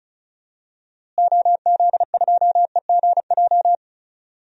Morse code, heard as a keyed tone, about 36 wpm, 700 Hz. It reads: OZ2EGJ
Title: DXCC-100 Call Signs - 6 Characters - Repeated 2x - 35wpm